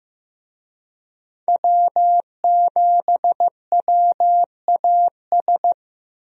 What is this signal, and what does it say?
Morse code, heard as a keyed tone, about 15 wpm, 700 Hz. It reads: W7WAS
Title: USA Call Signs - 5 Characters - Repeated 3x - 15wpm